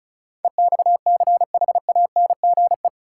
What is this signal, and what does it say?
Morse code, heard as a keyed tone, about 36 wpm, 700 Hz. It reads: EXCHANGE
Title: Sets of 5 Words - New Word Drill - 35wpm